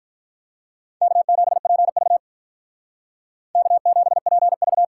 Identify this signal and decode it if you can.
Morse code, heard as a keyed tone, about 53 wpm, 700 Hz. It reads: K7PV K7PV